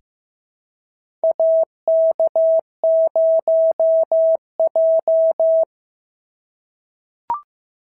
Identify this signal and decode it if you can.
Morse code, heard as a keyed tone, about 15 wpm, 650 Hz. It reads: AK0J
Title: SOTA QSO Practice 15wpm